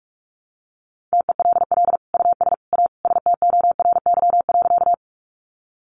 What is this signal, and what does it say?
Morse code, heard as a keyed tone, about 44 wpm, 700 Hz. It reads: TELL US A STORY.
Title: Sentences from Top 300 Words 45wpm